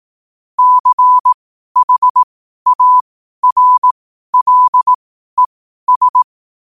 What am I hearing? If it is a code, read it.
Morse code, heard as a keyed tone, about 18 wpm, spaced as for 12 wpm, 1000 Hz. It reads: CHARLES
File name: qotdfiles / 12wpm